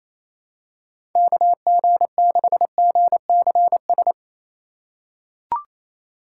Morse code, heard as a keyed tone, about 28 wpm, 700 Hz. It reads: KG6GCS
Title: USA Call Signs - 6 Characters - 4x Character Spacing - 28wpm